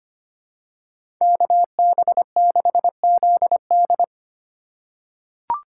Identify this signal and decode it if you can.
Morse code, heard as a keyed tone, about 25 wpm, 700 Hz. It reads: KB6ZD
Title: USA Call Signs - 5 Characters - 4x Character Spacing - 25wpm